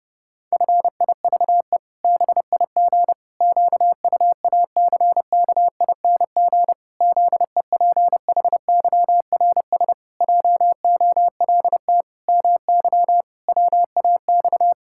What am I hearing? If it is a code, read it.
Morse code, heard as a keyed tone, about 30 wpm, 700 Hz. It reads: FIVE BIG QUACKING ZEPHYRS JOLT MY WAX
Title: Pangram Challenge v2 - 30wpm